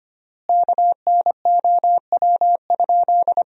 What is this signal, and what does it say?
Morse code, heard as a keyed tone, about 25 wpm, 700 Hz. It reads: KNOW?
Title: 3x Word-Spacing - Sentences from Top 100 Words - 25wpm